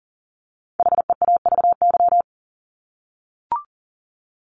Morse code, heard as a keyed tone, about 40 wpm, 700 Hz. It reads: HEAVY
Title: LY Suffix - Top 100 Words with Base Words 40wpm